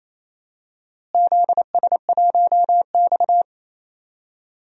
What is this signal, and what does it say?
Morse code, heard as a keyed tone, about 28 wpm, 700 Hz. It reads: ZS1X
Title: DXCC-100 Call Signs - 4 Characters - Repeated 2x - 28wpm